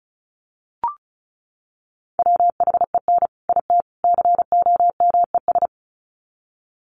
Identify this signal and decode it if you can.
Morse code, heard as a keyed tone, about 34 wpm, 700 Hz. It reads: WHEN IT COMES
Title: Sets of 3 Words 35wpm